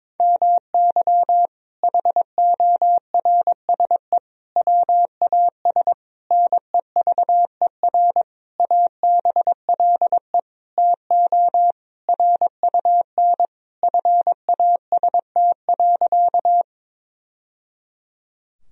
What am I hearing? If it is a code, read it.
Morse code, heard as a keyed tone, about 22 wpm, 700 Hz. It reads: MY HORSE WAS NEVER ABLE TO RUN FAST.
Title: Sentences from Top 1000 Words - Review Part 2 - 22wpm